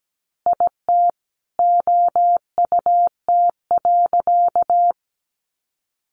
Morse code, heard as a keyed tone, about 17 wpm, 700 Hz. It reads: IT OUT.
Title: Sentences from Top 1500 Words 17wpm